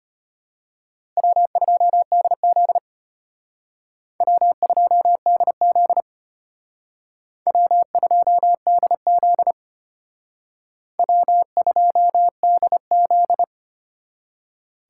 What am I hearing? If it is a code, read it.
Morse code, heard as a keyed tone, about 33 wpm, 700 Hz. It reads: W2DZ W2DZ W2DZ W2DZ